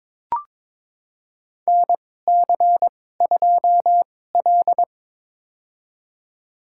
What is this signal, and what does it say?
Morse code, heard as a keyed tone, about 23 wpm, spaced as for 16 wpm, 700 Hz. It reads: NC2L
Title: USA Call Signs - 4 Characters - 2x Character Spacing - 22wpm